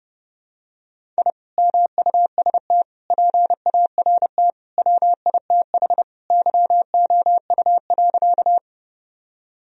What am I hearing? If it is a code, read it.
Morse code, heard as a keyed tone, about 32 wpm, 700 Hz. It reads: I MUST PART WITH YOU.
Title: Sentences from Top 1000 Words - Review Part 1 - 30wpm